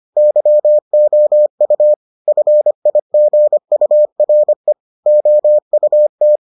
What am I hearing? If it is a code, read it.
Morse code, heard as a keyed tone, about 25 wpm, 600 Hz. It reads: YOU FIGURE OUT